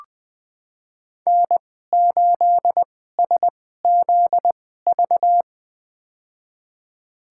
Morse code, heard as a keyed tone, about 20 wpm, spaced as for 14 wpm, 700 Hz. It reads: N8SZV
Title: USA Call Signs - 5 Characters - 2x Character Spacing - 20wpm